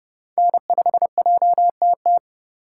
Morse code, heard as a keyed tone, about 31 wpm, 700 Hz. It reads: N5JTT